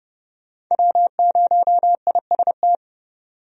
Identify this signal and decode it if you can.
Morse code, heard as a keyed tone, about 30 wpm, 700 Hz. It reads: W0IST
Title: USA Call Signs - 5 Characters - 30wpm